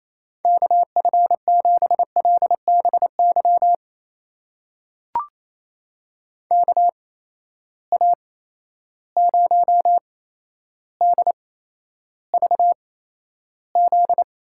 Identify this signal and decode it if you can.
Morse code, heard as a keyed tone, about 28 wpm, spaced as for 12 wpm, 700 Hz. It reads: KF7LBY KA0DVZ
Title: USA Call Signs - 6 Characters - 8x Character Spacing - 28wpm